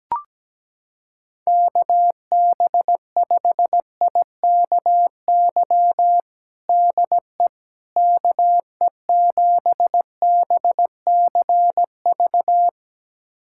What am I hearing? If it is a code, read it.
Morse code, heard as a keyed tone, about 17 wpm, 700 Hz. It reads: KB5IKY DE KE7BCV
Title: USA Callsign DE Callsign - 6 Characters - 17wpm